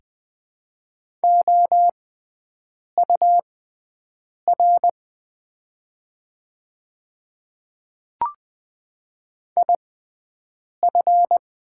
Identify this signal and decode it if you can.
Morse code, heard as a keyed tone, about 20 wpm, spaced as for 7 wpm, 700 Hz. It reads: OUR IF
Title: 6x Character Spacing - Top 200 Words - 20wpm